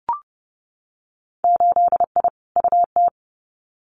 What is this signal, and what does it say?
Morse code, heard as a keyed tone, about 30 wpm, 700 Hz. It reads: 8I UT
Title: Winter Field Day Exchanges Encore 30wpm